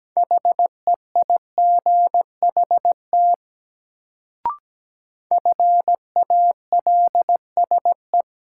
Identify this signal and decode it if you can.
Morse code, heard as a keyed tone, about 17 wpm, 700 Hz. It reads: HEIGHT FALSE
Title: Sentences from Top 1600 Words - New Word Drill Alt - 17wpm